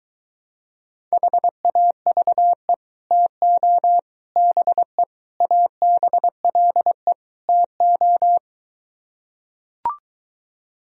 Morse code, heard as a keyed tone, about 24 wpm, 700 Hz. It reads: HAVE TO BE ABLE TO